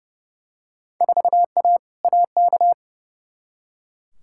Morse code, heard as a keyed tone, about 30 wpm, 700 Hz. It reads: 4A AK
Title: Field Day Exchanges - 30wpm